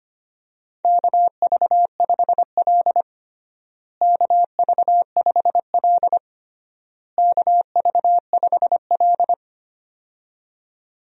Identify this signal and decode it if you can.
Morse code, heard as a keyed tone, about 25 wpm, 700 Hz. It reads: KV5L KV5L KV5L
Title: USA Call Signs - 4 Characters - Repeated 3x - 25wpm